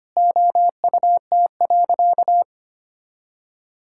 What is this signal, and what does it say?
Morse code, heard as a keyed tone, about 25 wpm, 700 Hz. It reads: OUT.
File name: practice / 25wpm / All-Age Appropriate Jokes - 25wpm